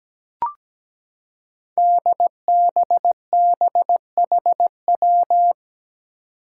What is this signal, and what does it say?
Morse code, heard as a keyed tone, about 17 wpm, 700 Hz. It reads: DBBHW
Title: Five Challenging Letter-Number Combinations 17wpm